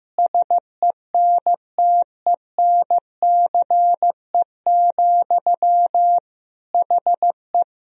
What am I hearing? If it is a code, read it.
Morse code, heard as a keyed tone, about 16 wpm, 700 Hz. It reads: SENTENCE, HE
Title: Sentences from Top 1000 Words - Review Part 1 - 15wpm